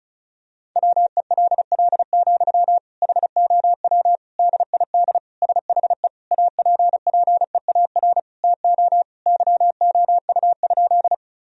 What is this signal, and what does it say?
Morse code, heard as a keyed tone, about 36 wpm, 700 Hz. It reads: WELL, HOW DID SHE APPEAR TO YOU?
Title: Sentences from Top 1000 Words - Review Part 2 - 35wpm